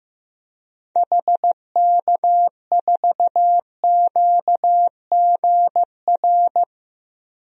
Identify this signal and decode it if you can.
Morse code, heard as a keyed tone, about 15 wpm, 700 Hz. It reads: HK4QGR
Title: DXCC-100 Call Signs - 6 Characters - 15wpm